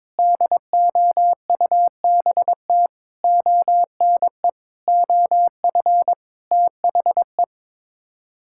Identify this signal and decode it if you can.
Morse code, heard as a keyed tone, about 22 wpm, 700 Hz. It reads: DOUBT ONE OF THE
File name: practice / 22wpm / Sets of 7 Words - 22wpm